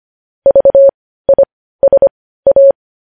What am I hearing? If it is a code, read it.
Morse code, heard as a keyed tone, about 25 wpm, spaced as for 15 wpm, 550 Hz. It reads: VISA
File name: podcast / News headlines-1x-25x15